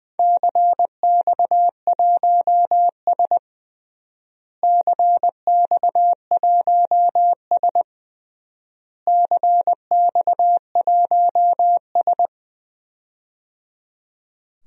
Morse code, heard as a keyed tone, about 20 wpm, 700 Hz. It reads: CX1S CX1S CX1S
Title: DXCC-100 Call Signs - 4 Characters - Repeated 3x - 20wpm